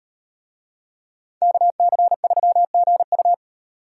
Morse code, heard as a keyed tone, about 38 wpm, 700 Hz. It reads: KC3GU